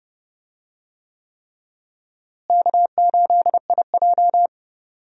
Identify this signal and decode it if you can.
Morse code, heard as a keyed tone, about 30 wpm, 700 Hz. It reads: K8IJ
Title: USA Call Signs - 4 Characters - Repeated 3x - 30wpm